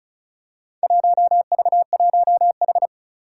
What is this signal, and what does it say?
Morse code, heard as a keyed tone, about 36 wpm, 700 Hz. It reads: 1V1H
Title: Four Challenging Letter-Number Combinations 35wpm